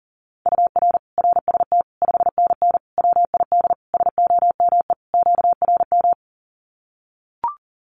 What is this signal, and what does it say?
Morse code, heard as a keyed tone, about 40 wpm, 700 Hz. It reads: UR RST 5NN WID SOME QRM
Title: Common QSO Phrases 40wpm